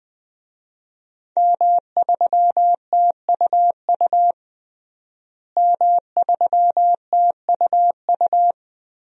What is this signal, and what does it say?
Morse code, heard as a keyed tone, about 20 wpm, 700 Hz. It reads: M3TUU M3TUU
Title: DXCC-100 Call Signs - 5 Characters - Repeated 2x - 20wpm